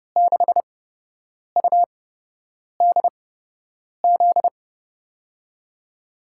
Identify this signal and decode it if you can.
Morse code, heard as a keyed tone, about 30 wpm, spaced as for 8 wpm, 700 Hz. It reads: 6UDZ
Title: USA Call Signs - 6 Characters - 8x Character Spacing - 30wpm